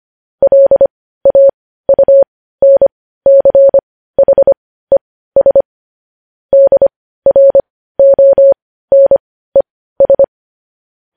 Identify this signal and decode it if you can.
Morse code, heard as a keyed tone, about 25 wpm, spaced as for 15 wpm, 550 Hz. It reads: LAUNCHES DRONES